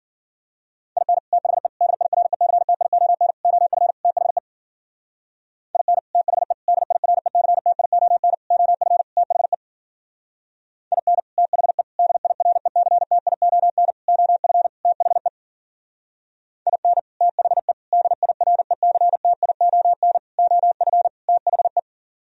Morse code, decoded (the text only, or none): IN THE DIRECTION OF THE IN THE DIRECTION OF THE IN THE DIRECTION OF THE IN THE DIRECTION OF THE